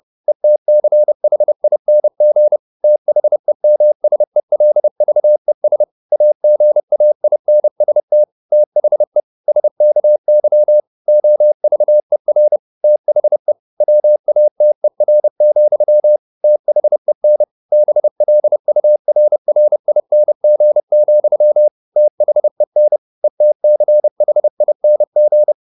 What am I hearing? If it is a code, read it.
Morse code, heard as a keyed tone, about 30 wpm, 600 Hz. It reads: ETCHING THEMSELVES AGAINST THE SKY OVER THE WATER, THEN BLURRING, THEN ETCHING
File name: the old man and the sea / the old man and the sea 1 30wpm